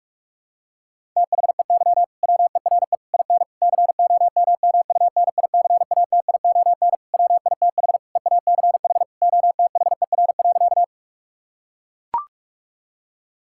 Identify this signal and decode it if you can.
Morse code, heard as a keyed tone, about 42 wpm, 700 Hz. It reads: THEY WERE IN COMMUNICATION WITH EACH OTHER.
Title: Sentences from Top 1000 Words - Review Part 6 - 45wpm